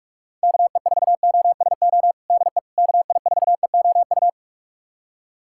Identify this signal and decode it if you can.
Morse code, heard as a keyed tone, about 46 wpm, 690 Hz. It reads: KE4OSO DE KI4EOU
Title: USA Callsign DE Callsign - 6 Characters - 45wpm